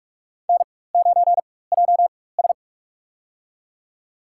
Morse code, heard as a keyed tone, about 47 wpm, spaced as for 21 wpm, 700 Hz. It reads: N9JS
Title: USA Call Signs - 4 Characters - 4x Character Spacing - 45wpm